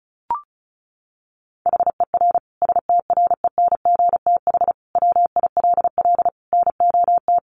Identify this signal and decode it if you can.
Morse code, heard as a keyed tone, about 34 wpm, 700 Hz. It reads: HER STRENGTH WILL NOT
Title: Sentences from Top 1300 Words 35wpm